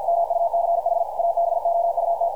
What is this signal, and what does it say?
Morse code, heard as a keyed tone, about 50 wpm, 680 Hz, with radio noise. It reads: MUCH ABOUT
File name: archived / CRSnet